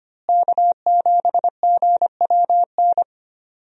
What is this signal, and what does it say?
Morse code, heard as a keyed tone, about 25 wpm, 700 Hz. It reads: K7GWN
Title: USA Call Signs - 5 Characters - Repeated 3x - 25wpm